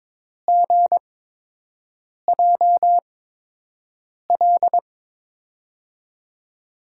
Morse code, heard as a keyed tone, about 22 wpm, spaced as for 6 wpm, 700 Hz. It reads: GJL
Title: USA Call Signs - 5 Characters - 8x Character Spacing - 22wpm